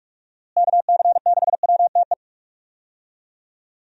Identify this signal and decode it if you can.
Morse code, heard as a keyed tone, about 41 wpm, 700 Hz. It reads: KC6WTE
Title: USA Call Signs - 6 Characters - Repeated 2x - 45wpm